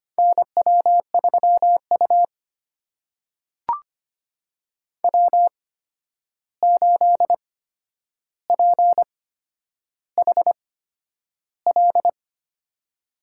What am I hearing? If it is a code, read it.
Morse code, heard as a keyed tone, about 25 wpm, spaced as for 10 wpm, 700 Hz. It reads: NW3U W8PHL
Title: USA Call Signs - 8x Character Spacing - 25wpm